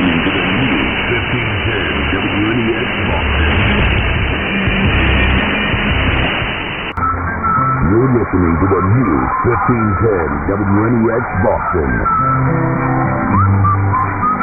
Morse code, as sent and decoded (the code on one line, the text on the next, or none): none